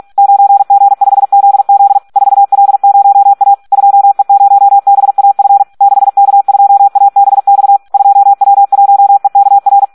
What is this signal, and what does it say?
Morse code, heard as a keyed tone, about 46 wpm, 800 Hz. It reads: 9G5ZZ 4L0A 2E0BAF 6K2ABX 2W1EKR
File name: Audio practica CW / SD 1 46